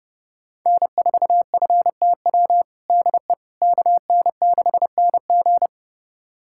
Morse code, heard as a keyed tone, about 30 wpm, 700 Hz. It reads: N4FTW DE KN6NG
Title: USA Callsign DE Callsign - 5 Characters - 30wpm